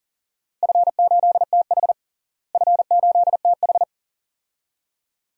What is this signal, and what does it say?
Morse code, heard as a keyed tone, about 40 wpm, 700 Hz. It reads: F8TH F8TH
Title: DXCC-100 Call Signs - 4 Characters - Repeated 2x - 40wpm